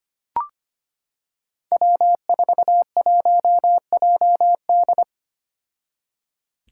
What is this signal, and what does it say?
Morse code, heard as a keyed tone, about 25 wpm, 700 Hz. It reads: W41JD
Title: Five Challenging Letter-Number Combinations 25wpm